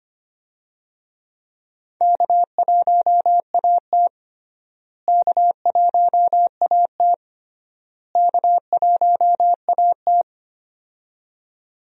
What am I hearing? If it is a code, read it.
Morse code, heard as a keyed tone, about 25 wpm, 700 Hz. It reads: K1AT K1AT K1AT
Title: USA Call Signs - 4 Characters - Repeated 3x - 25wpm